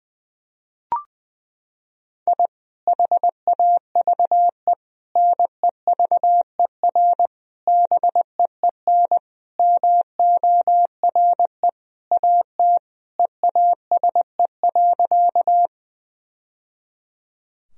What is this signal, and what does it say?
Morse code, heard as a keyed tone, about 21 wpm, 700 Hz. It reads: I HAVE NEVER BEEN MORE AT EASE.
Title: Sentences from Top 1000 Words - Review Part 1 - 20wpm